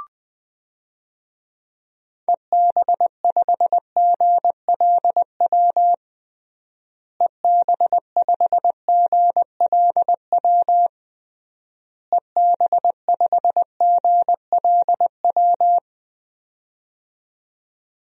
Morse code, heard as a keyed tone, about 20 wpm, 700 Hz. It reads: EB5GLW EB5GLW EB5GLW
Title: DXCC-100 Call Signs - 6 Characters - Repeated 3x - 20wpm